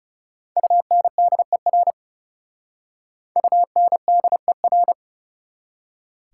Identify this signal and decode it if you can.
Morse code, heard as a keyed tone, about 39 wpm, 700 Hz. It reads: UNDER UNDER